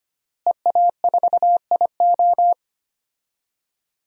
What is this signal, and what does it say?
Morse code, heard as a keyed tone, about 25 wpm, 700 Hz. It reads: EA4IO
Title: DXCC-100 Call Signs - 5 Characters - Repeated 2x - 25wpm